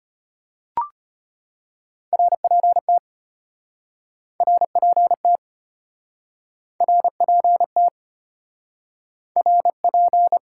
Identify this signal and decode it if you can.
Morse code, heard as a keyed tone, about 33 wpm, 700 Hz. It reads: RPT RPT RPT RP